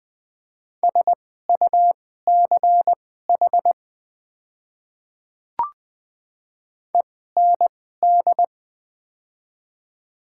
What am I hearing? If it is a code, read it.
Morse code, heard as a keyed tone, about 20 wpm, spaced as for 14 wpm, 700 Hz. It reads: SUCH END
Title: 2x Character Spacing - Top 200 Words - 20wpm